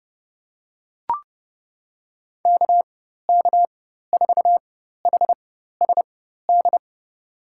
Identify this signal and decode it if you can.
Morse code, heard as a keyed tone, about 30 wpm, spaced as for 14 wpm, 690 Hz. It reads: KK4HSD
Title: USA Call Signs - 4x Character Spacing - 30wpm